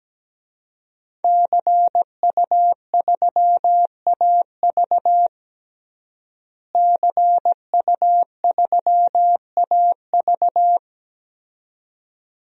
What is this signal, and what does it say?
Morse code, heard as a keyed tone, about 17 wpm, 700 Hz. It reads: CU3AV CU3AV
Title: DXCC-100 Call Signs - 5 Characters - Repeated 2x - 17wpm